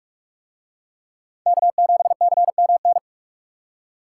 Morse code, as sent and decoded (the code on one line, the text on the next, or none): -.- --... -.-. -- -.
K7CMN